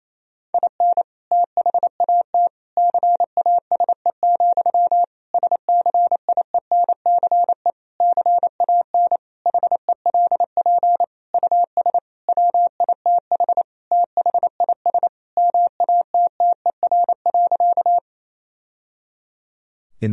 Morse code, heard as a keyed tone, about 28 wpm, 700 Hz. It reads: IN THAT CASE, SCIENCE CAN HELP US WITH THIS MATTER.
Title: Sentences from Top 1000 Words - Review Part 2 - 28wpm